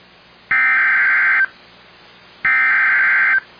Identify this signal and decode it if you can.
Emergency Alert System (SAME) tones (header bursts), heard as harsh buzzing bursts. It is heard from 0.5 to 3.3 s.